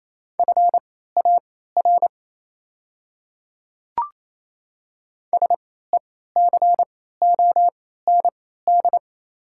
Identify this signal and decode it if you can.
Morse code, heard as a keyed tone, about 28 wpm, spaced as for 16 wpm, 690 Hz. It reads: FAR SECOND
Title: 3x Character Spacing - Top 300 Words - 28wpm